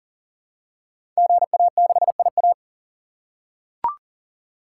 Morse code, heard as a keyed tone, about 42 wpm, 700 Hz. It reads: GA6IA
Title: Code Groups - 5 Letter:Numbers - 40wpm